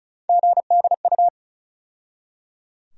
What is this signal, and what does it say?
Morse code, heard as a keyed tone, about 36 wpm, 700 Hz. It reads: GDU
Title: USA Call Sign Postfixes - 3 Letters 35wpm